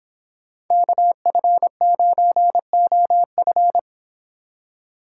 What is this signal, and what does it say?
Morse code, heard as a keyed tone, about 26 wpm, 700 Hz. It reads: KF9OF